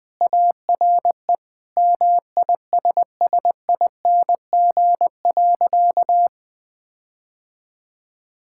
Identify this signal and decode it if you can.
Morse code, heard as a keyed tone, about 21 wpm, 700 Hz. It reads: ARE MISSING.
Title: Sentences with 5-Word Phrases - Spoken Beginning - 20wpm